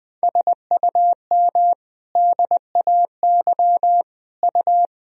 Morse code, heard as a keyed tone, about 21 wpm, 700 Hz. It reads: SUM DAY U
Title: Sentences from Top 200 Words and Abbreviations 20wpm